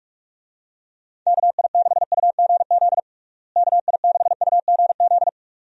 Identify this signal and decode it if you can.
Morse code, heard as a keyed tone, about 46 wpm, 700 Hz. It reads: KI6UGZ KI6UGZ
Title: USA Call Signs - 6 Characters - Repeated 2x - 45wpm